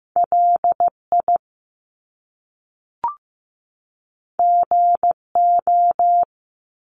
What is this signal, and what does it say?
Morse code, heard as a keyed tone, about 15 wpm, 700 Hz. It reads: LI GO
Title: Top Two Letter Combinations in Words 15wpm